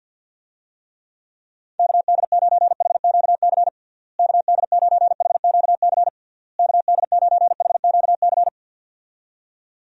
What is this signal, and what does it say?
Morse code, heard as a keyed tone, about 52 wpm, 700 Hz. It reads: KD9HQC KD9HQC KD9HQC
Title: USA Call Signs - 6 Characters - Repeated 3x - 50wpm